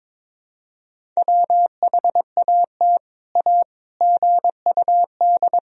Morse code, heard as a keyed tone, about 23 wpm, 700 Hz. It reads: WHAT A GUD
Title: Sentences from Top 200 Words and Abbreviations 22wpm